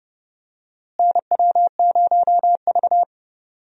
Morse code, heard as a keyed tone, about 32 wpm, 700 Hz. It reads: NW0V
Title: USA Call Signs - 4 Characters - Repeated 2x - 30wpm